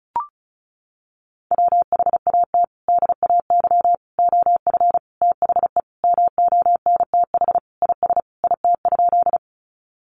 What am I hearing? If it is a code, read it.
Morse code, heard as a keyed tone, about 34 wpm, 700 Hz. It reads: WHAT DAY OF THE MONTH IS IT?
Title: Sentences from Top 800 Words 35wpm